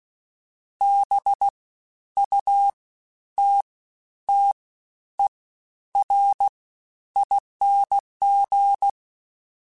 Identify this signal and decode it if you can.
Morse code, heard as a keyed tone, about 16 wpm, spaced as for 10 wpm, 800 Hz. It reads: BUTTERING
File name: Special INGs-mp3s / Special INGs-mp3s / SlowToFaster ing @15